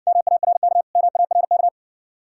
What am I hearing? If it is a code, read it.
Morse code, heard as a keyed tone, about 61 wpm, 700 Hz. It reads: MARK MARK